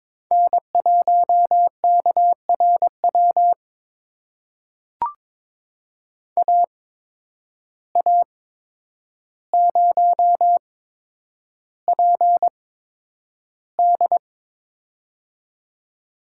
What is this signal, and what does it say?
Morse code, heard as a keyed tone, about 22 wpm, spaced as for 9 wpm, 700 Hz. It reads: N1KRW AA0PD
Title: USA Call Signs - 5 Characters - 8x Character Spacing - 22wpm